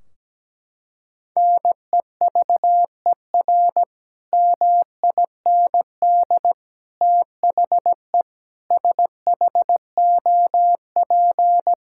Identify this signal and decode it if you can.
Morse code, heard as a keyed tone, about 17 wpm, 700 Hz. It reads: NEVER MIND THE SHOP
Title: Sentences from Top 1600 Words 17wpm